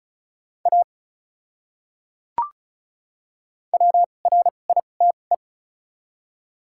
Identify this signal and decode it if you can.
Morse code, heard as a keyed tone, about 34 wpm, spaced as for 25 wpm, 700 Hz. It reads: A WRITE
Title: 2x Character Spacing - Top 300 Words - 35wpm